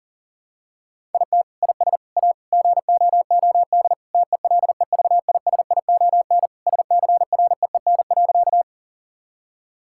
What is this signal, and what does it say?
Morse code, heard as a keyed tone, about 40 wpm, 700 Hz. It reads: IT IS A GOOD TELEVISION SCREEN.
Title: Sentences from Top 1000 Words - Review Part 6 - 40wpm